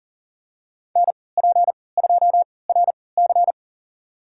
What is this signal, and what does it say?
Morse code, heard as a keyed tone, about 40 wpm, spaced as for 23 wpm, 700 Hz. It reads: NP2RC
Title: USA Call Signs - 5 Characters - 3x Character Spacing - 40wpm